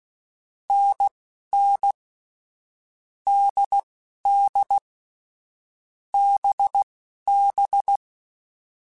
Morse code, heard as a keyed tone, about 16 wpm, spaced as for 11 wpm, 800 Hz. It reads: NN DD BB
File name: Short Runs / Short Runs-mp3s / NDB NDB6 @10